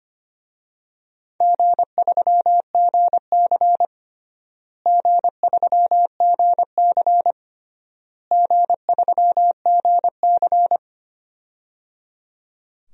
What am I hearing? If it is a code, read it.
Morse code, heard as a keyed tone, about 25 wpm, 700 Hz. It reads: G3GC G3GC G3GC
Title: DXCC-100 Call Signs - 4 Characters - Repeated 3x - 25wpm